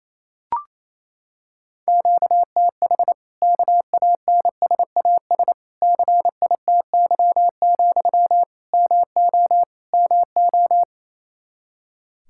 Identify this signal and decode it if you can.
Morse code, heard as a keyed tone, about 28 wpm, 700 Hz. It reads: QTH KANSAS CITY, MO MO
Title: Practice QSO - Exchange 1 - Beginning - 28wpm